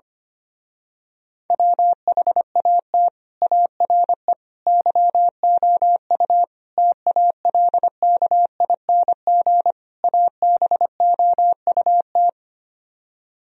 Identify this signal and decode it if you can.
Morse code, heard as a keyed tone, about 24 wpm, 700 Hz. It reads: WHAT ARE YOU TALKING ABOUT
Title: Speed-Racing - Sets of 5 Words 25wpm